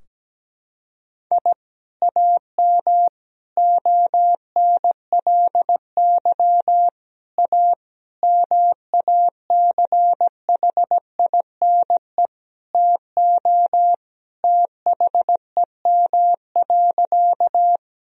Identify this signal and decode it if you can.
Morse code, heard as a keyed tone, about 17 wpm, 700 Hz. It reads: I AM ONLY A MACHINE TO THEM.
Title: Sentences from Top 1000 Words - Review Part 2 - 17wpm